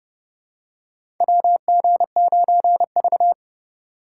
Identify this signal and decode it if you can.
Morse code, heard as a keyed tone, about 30 wpm, 700 Hz. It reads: WG9V
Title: USA Call Signs - 4 Characters - 4x Character Spacing - 30wpm